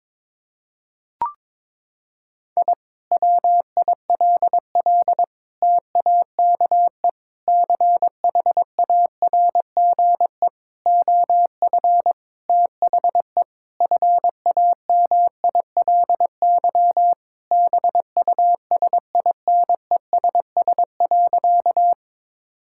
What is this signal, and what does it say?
Morse code, heard as a keyed tone, about 22 wpm, 700 Hz. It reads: I WILL TAKE CHARGE OF THE FAMILY BUSINESS.
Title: Sentences from Top 1000 Words - Review Part 6 - 22wpm